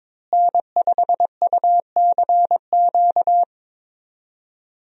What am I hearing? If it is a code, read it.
Morse code, heard as a keyed tone, about 21 wpm, 700 Hz. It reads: N5UCQ
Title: USA Call Signs - 5 Characters - Repeated 3x - 22wpm